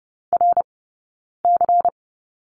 Morse code, heard as a keyed tone, about 30 wpm, 700 Hz. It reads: R C
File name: 092 Lesson 16d - Letter c - Two Characters 30wpm